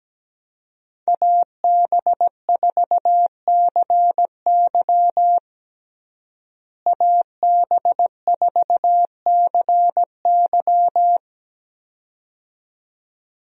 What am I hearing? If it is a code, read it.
Morse code, heard as a keyed tone, about 17 wpm, 700 Hz. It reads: AB4CY AB4CY
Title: USA Call Signs - 5 Characters - Repeated 2x - 17wpm